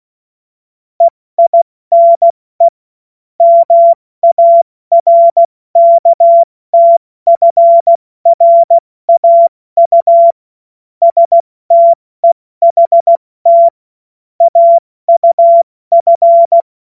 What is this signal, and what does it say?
Morse code, heard as a keyed tone, about 16 wpm, 680 Hz. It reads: EINE MARKTFRAU STEHT AUF
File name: Morsekurs / Witze / Witz 08. August 2023 70 F80